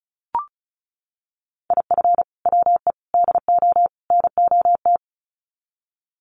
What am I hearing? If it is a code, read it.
Morse code, heard as a keyed tone, about 34 wpm, 700 Hz. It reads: IF WE DO NOT
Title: Sets of 4 Words 35wpm